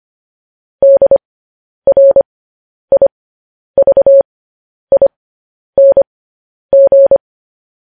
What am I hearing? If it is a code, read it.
Morse code, heard as a keyed tone, about 25 wpm, spaced as for 10 wpm, 550 Hz. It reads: DRIVING
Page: news headlines in morse code at 10 wpm